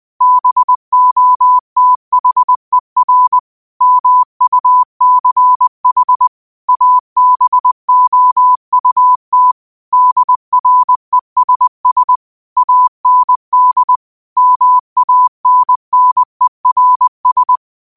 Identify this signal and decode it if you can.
Morse code, heard as a keyed tone, about 20 wpm, 1000 Hz. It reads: BOTHER MUCH ABOUT DRESS AND MANNERS
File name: qotdfiles / 20wpm